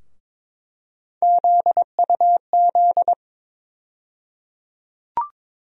Morse code, heard as a keyed tone, about 21 wpm, 700 Hz. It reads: ZUZ
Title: USA Call Sign Postfixes - 3 Letters 22wpm